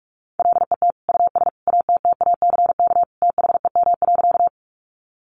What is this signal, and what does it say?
Morse code, heard as a keyed tone, about 44 wpm, 700 Hz. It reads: LET US ATTACK THEM.
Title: Sentences from Top 1500 Words 45wpm